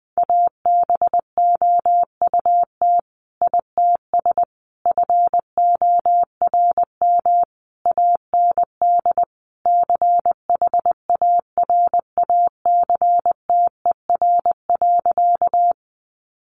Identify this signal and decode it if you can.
Morse code, heard as a keyed tone, about 20 wpm, 700 Hz. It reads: ABOUT ITS FORM AND CHARACTER.
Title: Sentences from Top 1200 Words 20wpm